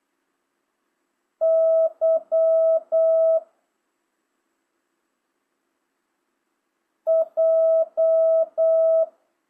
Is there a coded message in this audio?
Yes: Morse code from 1.4 to 9.0 s, heard as a keyed tone, about 8 wpm, 650 Hz. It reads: Y J